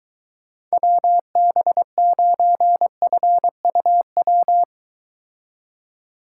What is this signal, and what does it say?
Morse code, heard as a keyed tone, about 24 wpm, 700 Hz. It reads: WB9FUW